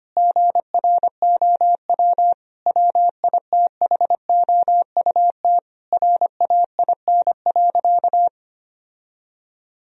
Morse code, heard as a keyed tone, about 25 wpm, 700 Hz. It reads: GROW WITHOUT RAIN.
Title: Sentences from Top 1000 Words - Review Part 3 - 25wpm